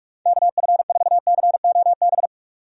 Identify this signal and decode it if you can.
Morse code, heard as a keyed tone, about 46 wpm, 700 Hz. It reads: KF4COB
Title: USA Call Signs - 6 Characters - 45wpm